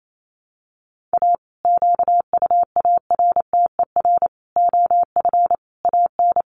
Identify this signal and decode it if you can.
Morse code, heard as a keyed tone, about 28 wpm, 700 Hz. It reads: A QUARTER OF AN
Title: Sentences from Top 1300 Words 28wpm